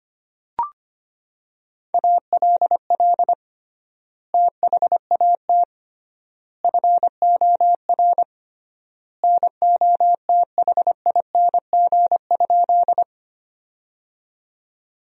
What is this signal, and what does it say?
Morse code, heard as a keyed tone, about 25 wpm, 700 Hz. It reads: ALL THAT FOR NOTHING?
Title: 3x Word-Spacing - Sentences from Top 500 Words - 25wpm